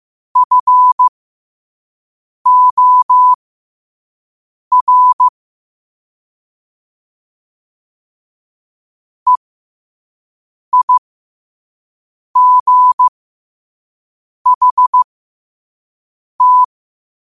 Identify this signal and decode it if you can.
Morse code, heard as a keyed tone, about 15 wpm, spaced as for 5 wpm, 1000 Hz. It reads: FOR EIGHT
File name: qotdfiles / 5wpm